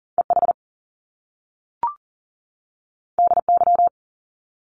Morse code, heard as a keyed tone, about 42 wpm, 700 Hz. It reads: EH DY
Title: Top Two Letter Combinations in Words 40wpm